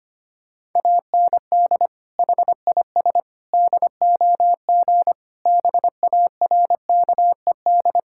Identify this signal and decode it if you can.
Morse code, heard as a keyed tone, about 25 wpm, 700 Hz. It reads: AND HIS DOG BARKED
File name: practice / 25wpm / Sentences with 2-Word Phrases - 25wpm